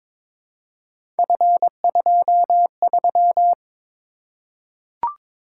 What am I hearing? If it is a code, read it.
Morse code, heard as a keyed tone, about 22 wpm, 700 Hz. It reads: F23
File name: practice / 22wpm / Three Letter-Number Combinations - 22wpm